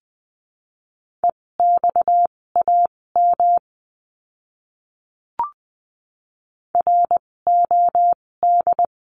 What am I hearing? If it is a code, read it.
Morse code, heard as a keyed tone, about 20 wpm, spaced as for 16 wpm, 700 Hz. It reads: EXAM ROD
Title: 256 Lesson 42a - Words 12-20wpm